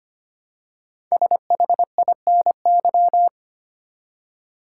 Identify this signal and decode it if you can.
Morse code, heard as a keyed tone, about 25 wpm, 700 Hz. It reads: SHINY
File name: 5 Letter Words - First 3 Letters Given - 25wpm